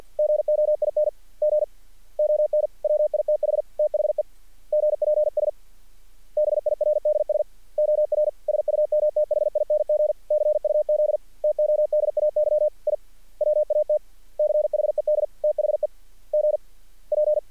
Morse code, heard as a keyed tone, about 50 wpm, spaced as for 43 wpm, 600 Hz. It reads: GOING ON WITH THE GPS BIRDS OR SUMTHING CUZ TODAY I WATCHED THE GP
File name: archived / CRSnet